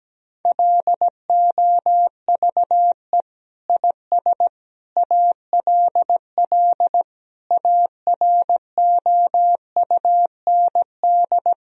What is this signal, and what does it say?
Morse code, heard as a keyed tone, about 17 wpm, 700 Hz. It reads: LOVE IS ALL AROUND
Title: Sentences from Top 1000 Words - Review Part 3 - 17wpm